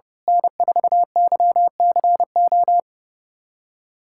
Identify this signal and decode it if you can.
Morse code, heard as a keyed tone, about 30 wpm, 700 Hz. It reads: N4YCO
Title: USA Callsign DE Callsign - 5 Characters - 30wpm